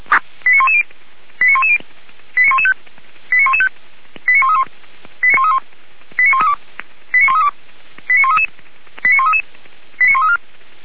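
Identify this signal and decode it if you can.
Five-tone selective call (ZVEI), heard as quick runs of five tones, tones of about 70 ms. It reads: 89110 89110 89115 89115 89131 89131 89132 89132 89133 89133 89134
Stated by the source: Received on an SDR as a ZVEI five-tone call, code 89109.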